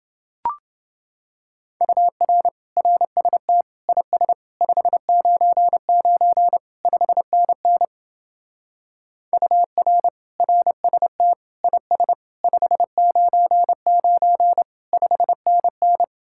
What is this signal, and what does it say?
Morse code, heard as a keyed tone, about 30 wpm, 700 Hz. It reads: UR RST IS 599 5NN UR RST IS 599 5NN